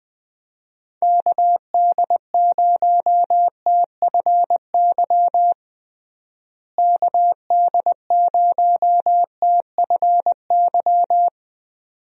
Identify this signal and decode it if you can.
Morse code, heard as a keyed tone, about 20 wpm, 700 Hz. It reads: KD0TFY KD0TFY
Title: USA Call Signs - 6 Characters - Repeated 2x - 20wpm